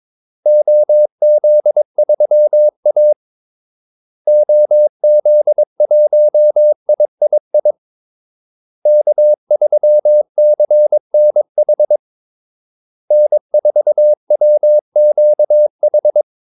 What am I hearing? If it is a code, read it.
Morse code, heard as a keyed tone, about 22 wpm, 600 Hz. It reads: OZ3A OZ1III K3CNH N4WQH
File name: Callsigns-22wpm